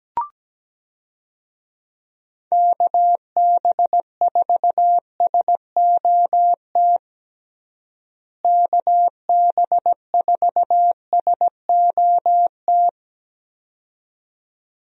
Morse code, heard as a keyed tone, about 17 wpm, 700 Hz. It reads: KB4SOT KB4SOT
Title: USA Call Signs - 6 Characters - Repeated 2x - 17wpm